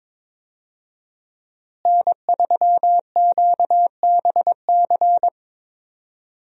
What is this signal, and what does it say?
Morse code, heard as a keyed tone, about 21 wpm, 700 Hz. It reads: N3QBC
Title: USA Call Signs - 5 Characters - Repeated 2x - 22wpm